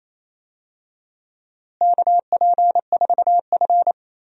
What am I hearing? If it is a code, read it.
Morse code, heard as a keyed tone, about 28 wpm, 700 Hz. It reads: KP4F
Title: DXCC-100 Call Signs - 4 Characters - Repeated 2x - 28wpm